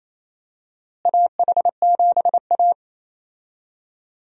Morse code, heard as a keyed tone, about 28 wpm, 700 Hz. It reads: AH7A
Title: USA Call Signs - 4 Characters - Repeated 2x - 28wpm